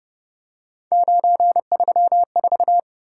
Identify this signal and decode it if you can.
Morse code, heard as a keyed tone, about 32 wpm, 700 Hz. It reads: 934